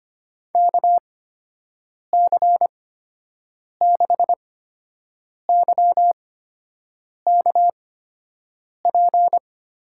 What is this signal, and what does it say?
Morse code, heard as a keyed tone, about 25 wpm, spaced as for 7 wpm, 700 Hz. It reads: KC6YKP
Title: USA Call Signs - 6 Characters - 8x Character Spacing - 25wpm